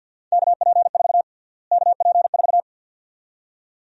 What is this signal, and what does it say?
Morse code, heard as a keyed tone, about 52 wpm, 700 Hz. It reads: KP4 KP4
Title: USA Call Sign Prefixes - Repeated 2x - 50wpm